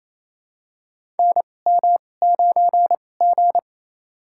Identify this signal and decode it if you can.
Morse code, heard as a keyed tone, about 28 wpm, spaced as for 20 wpm, 700 Hz. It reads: NM9G
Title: USA Call Signs - 4 Characters - 2x Character Spacing - 28wpm